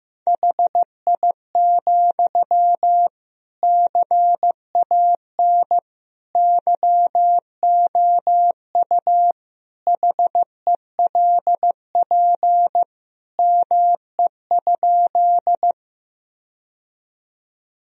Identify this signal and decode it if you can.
Morse code, heard as a keyed tone, about 15 wpm, 700 Hz. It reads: HI, CAN YOU HELP ME?
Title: Sentences from Top 1000 Words - Review Part 3 - 15wpm